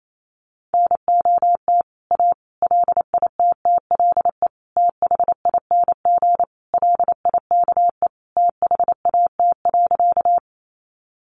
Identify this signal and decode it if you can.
Morse code, heard as a keyed tone, about 28 wpm, 700 Hz. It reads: NOT A LITTLE THING LIKE THAT.
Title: A Sound Of Thunder 28wpm